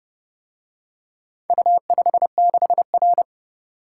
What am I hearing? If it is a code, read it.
Morse code, heard as a keyed tone, about 29 wpm, 700 Hz. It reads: U56R